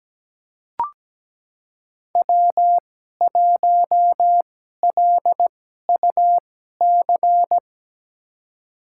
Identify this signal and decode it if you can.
Morse code, heard as a keyed tone, about 17 wpm, spaced as for 12 wpm, 700 Hz. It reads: W1LUC
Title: USA Call Signs - 2x Character Spacing - 17wpm